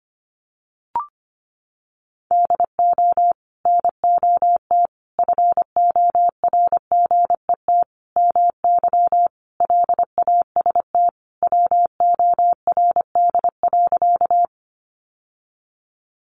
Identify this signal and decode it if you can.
Morse code, heard as a keyed tone, about 25 wpm, 700 Hz. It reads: DO NOT FORGET MY LAST WORD.
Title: Sentences from Top 1200 Words 25wpm